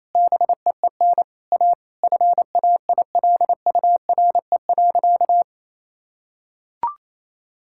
Morse code, heard as a keyed tone, about 28 wpm, 700 Hz. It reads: BEEN A FAILURE.
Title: Sentences from Top 1000 Words - Review Part 6 - 28wpm